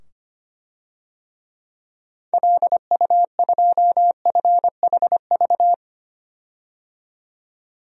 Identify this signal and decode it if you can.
Morse code, heard as a keyed tone, about 25 wpm, 700 Hz. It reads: LU2FHV
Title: DXCC-100 Call Signs - 6 Characters - 25wpm